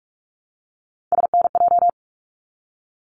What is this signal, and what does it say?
Morse code, heard as a keyed tone, about 44 wpm, 700 Hz. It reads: SNJ
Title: ARRL Sections 45wpm